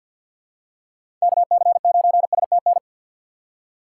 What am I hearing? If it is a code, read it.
Morse code, heard as a keyed tone, about 52 wpm, 700 Hz. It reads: KC9STN
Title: USA Call Signs - 6 Characters - 3x Character Spacing - 50wpm